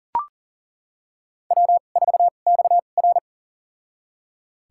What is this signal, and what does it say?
Morse code, heard as a keyed tone, about 40 wpm, spaced as for 29 wpm, 700 Hz. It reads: W4XR
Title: USA Call Signs - 2x Character Spacing - 40wpm